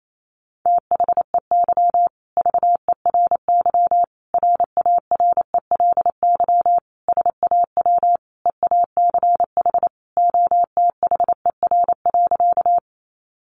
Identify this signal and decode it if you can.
Morse code, heard as a keyed tone, about 28 wpm, 700 Hz. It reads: THEY VERY RARELY SAW EACH OTHER.
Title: Sentences from Top 1400 Words 28wpm